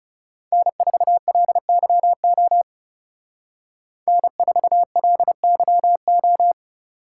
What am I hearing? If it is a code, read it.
Morse code, heard as a keyed tone, about 37 wpm, 700 Hz. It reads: N4LYO N4LYO
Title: Speed Racing - USA 1x3 Call Signs 30wpm